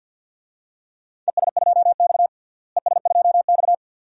Morse code, heard as a keyed tone, about 48 wpm, 700 Hz. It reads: ES2X ES2X
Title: DXCC-100 Call Signs - 4 Characters - Repeated 2x - 50wpm